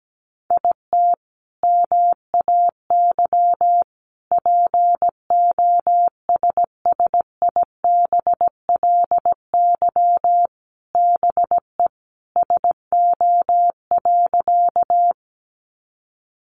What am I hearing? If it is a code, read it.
Morse code, heard as a keyed tone, about 17 wpm, 700 Hz. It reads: IT MAY POSSIBLY BE SO.
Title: Sentences from Top 1300 Words 17wpm